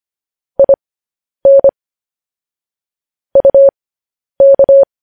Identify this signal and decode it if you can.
Morse code, heard as a keyed tone, about 25 wpm, spaced as for 10 wpm, 550 Hz. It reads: IN UK
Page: news-headlines-in-morse-code-at-10-wpm